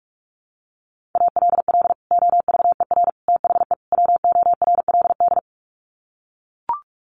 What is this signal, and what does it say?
Morse code, heard as a keyed tone, about 44 wpm, 700 Hz. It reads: ALL OVER THE WORLD
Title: Sets of 4 Words 45wpm